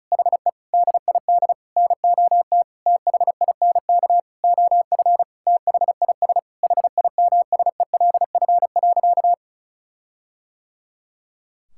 Morse code, heard as a keyed tone, about 34 wpm, 700 Hz. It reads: HE DID NOT THINK OF THIS HIMSELF.
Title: Sentences from Top 1600 Words 35wpm